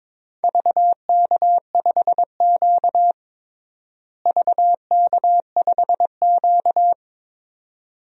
Morse code, heard as a keyed tone, about 21 wpm, 700 Hz. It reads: VK5Q VK5Q
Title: DXCC-100 Call Signs - 4 Characters - Repeated 2x - 22wpm